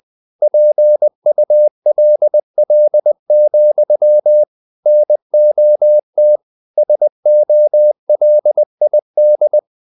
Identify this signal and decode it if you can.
Morse code, heard as a keyed tone, about 20 wpm, 600 Hz. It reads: PULL, NOT SOLID
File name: the old man and the sea / the old man and the sea part 1 20wpm